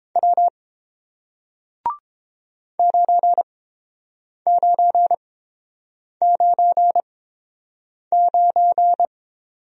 Morse code, heard as a keyed tone, about 30 wpm, spaced as for 7 wpm, 700 Hz. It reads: W 9999